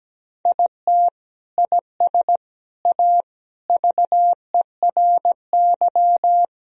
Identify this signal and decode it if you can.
Morse code, heard as a keyed tone, about 17 wpm, 700 Hz. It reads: IT IS A VERY
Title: Sentences from Top 1000 Words - Review Part 1 - 17wpm